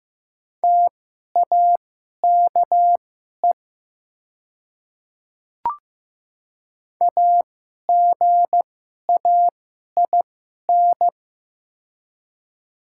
Morse code, heard as a keyed tone, about 15 wpm, spaced as for 11 wpm, 700 Hz. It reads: TAKE AGAIN
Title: 2x Character Spacing - Top 300 Words - 15wpm